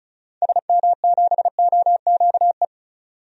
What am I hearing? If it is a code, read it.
Morse code, heard as a keyed tone, about 36 wpm, 700 Hz. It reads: SM7OQE
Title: DXCC-100 Call Signs - 6 Characters - 35wpm